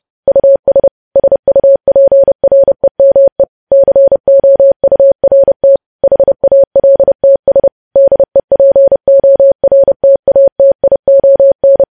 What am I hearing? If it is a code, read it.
Morse code, heard as a keyed tone, about 30 wpm, 550 Hz. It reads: US SUPREME COURT HALTS DEPORTATION
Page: news headlines in morse code at 30 wpm